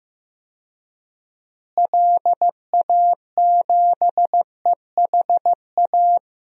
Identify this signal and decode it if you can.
Morse code, heard as a keyed tone, about 15 wpm, 700 Hz. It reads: LA7EHA
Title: DXCC-100 Call Signs - 6 Characters - Repeated 3x - 15wpm